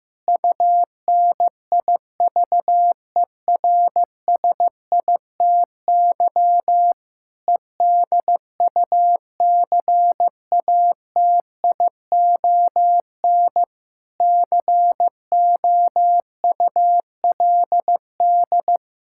Morse code, heard as a keyed tone, about 15 wpm, 700 Hz. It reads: UNIVERSITY EDUCATION COULD
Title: Sentences from Top 1000 Words - Review Part 6 - 15wpm